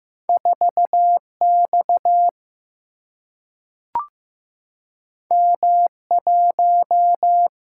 Morse code, heard as a keyed tone, about 15 wpm, 700 Hz. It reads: 4X M1
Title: Single-Letter Single-Number 15wpm